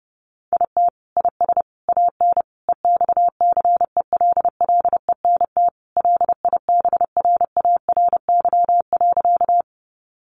Morse code, heard as a keyed tone, about 30 wpm, 700 Hz. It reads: IT IS AN EXCELLENT LIBRARY.
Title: Sentences from Top 1000 Words 30wpm